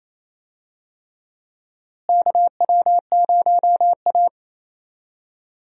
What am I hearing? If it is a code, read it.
Morse code, heard as a keyed tone, about 28 wpm, 700 Hz. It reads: KW0A
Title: USA Call Signs - 4 Characters - Repeated 3x - 28wpm